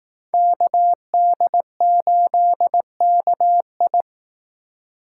Morse code, heard as a keyed tone, about 18 wpm, 700 Hz. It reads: KD8KI
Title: Speed Racing - USA 2x2 Call Signs 15wpm